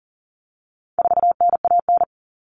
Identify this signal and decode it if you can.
Morse code, heard as a keyed tone, about 40 wpm, 700 Hz. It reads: 4NAN
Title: Common Cut Numbers - 4-Digit 40wpm